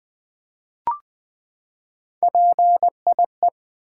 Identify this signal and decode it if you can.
Morse code, heard as a keyed tone, about 21 wpm, 700 Hz. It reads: PIE
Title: Three Letter Words - 20wpm